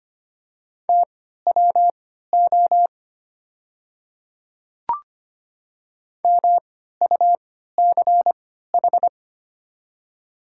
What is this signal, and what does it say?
Morse code, heard as a keyed tone, about 25 wpm, spaced as for 14 wpm, 700 Hz. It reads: TWO MUCH
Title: 3x Character Spacing - Top 200 Words - 25wpm